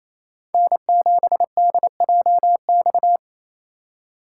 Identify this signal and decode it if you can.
Morse code, heard as a keyed tone, about 28 wpm, 700 Hz. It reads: N7DJX
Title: USA Call Signs - 5 Characters - 2x Character Spacing - 28wpm